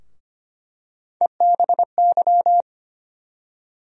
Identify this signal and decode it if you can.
Morse code, heard as a keyed tone, about 25 wpm, 700 Hz. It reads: EBY
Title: USA Call Sign Postfixes - 3 Letters 25wpm